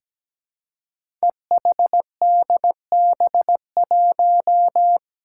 Morse code, heard as a keyed tone, about 17 wpm, 700 Hz. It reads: EHDB1
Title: Five Challenging Letter-Number Combinations 17wpm